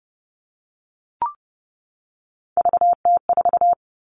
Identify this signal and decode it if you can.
Morse code, heard as a keyed tone, about 30 wpm, 700 Hz. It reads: VT4
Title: All Cut Numbers - 3-Digit 30wpm